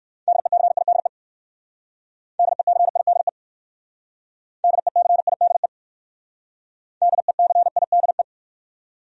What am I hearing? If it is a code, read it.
Morse code, heard as a keyed tone, about 57 wpm, 700 Hz. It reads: DECIDE DECIDE DECIDE DECIDE